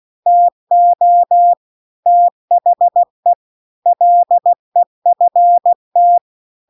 Morse code, heard as a keyed tone, about 16 wpm, 700 Hz. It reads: TO THE LEFT